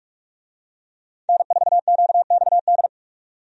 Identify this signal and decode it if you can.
Morse code, heard as a keyed tone, about 46 wpm, 700 Hz. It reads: N4QXD
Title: USA Call Signs - 5 Characters - 45wpm